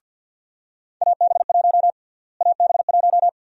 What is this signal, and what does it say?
Morse code, heard as a keyed tone, about 46 wpm, 700 Hz. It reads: AB1 AB1
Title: USA Call Sign Prefixes - Repeated 2x - 50wpm